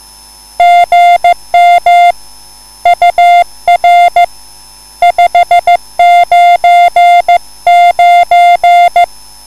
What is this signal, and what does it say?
Morse code, heard as a keyed tone, about 15 wpm, 700 Hz. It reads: GM UR 599
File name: QSL / CW lesson